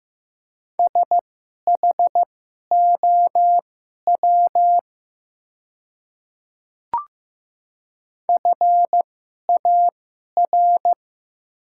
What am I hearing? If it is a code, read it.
Morse code, heard as a keyed tone, about 15 wpm, spaced as for 11 wpm, 700 Hz. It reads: SHOW FAR
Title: 2x Character Spacing - Top 300 Words - 15wpm